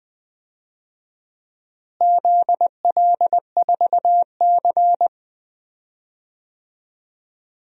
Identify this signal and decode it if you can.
Morse code, heard as a keyed tone, about 20 wpm, 700 Hz. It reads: ZL4C
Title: DXCC-100 Call Signs - 4 Characters - Repeated 2x - 20wpm